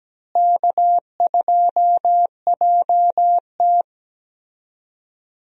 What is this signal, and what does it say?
Morse code, heard as a keyed tone, about 17 wpm, 700 Hz. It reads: K2JT
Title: USA Call Signs - 4 Characters - Repeated 3x - 17wpm